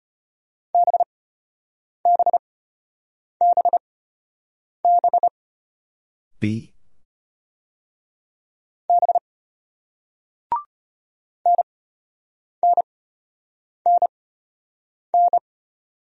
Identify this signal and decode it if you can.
Morse code, heard as a keyed tone, about 33 wpm, spaced as for 8 wpm, 700 Hz. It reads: BBBB B NNNN